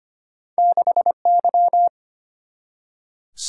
Morse code, heard as a keyed tone, about 25 wpm, 700 Hz. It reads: 6Y
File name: Two Letter-Number Combinations - 25wpm